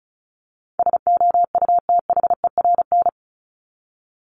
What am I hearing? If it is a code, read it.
Morse code, heard as a keyed tone, about 34 wpm, 700 Hz. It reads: SOUTHERN
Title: Top 1751-2000 Words 35wpm